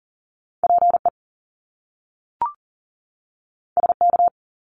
Morse code, heard as a keyed tone, about 41 wpm, 700 Hz. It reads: PE SK
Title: Canadian Provinces and Territories 40wpm